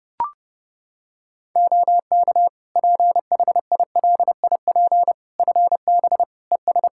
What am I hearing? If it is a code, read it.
Morse code, heard as a keyed tone, about 30 wpm, 700 Hz. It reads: OK PHILIP FB ES
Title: K1USN SST QSO Practice 30wpm